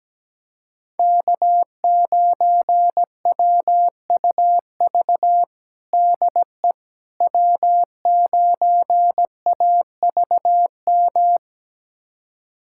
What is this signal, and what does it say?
Morse code, heard as a keyed tone, about 17 wpm, 700 Hz. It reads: K9WUV DE W9AVM
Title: USA Callsign DE Callsign - 5 Characters - 17wpm